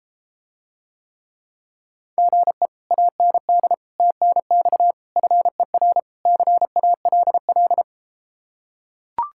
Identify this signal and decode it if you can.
Morse code, heard as a keyed tone, about 33 wpm, 700 Hz. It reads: GE AND TNX FER CALL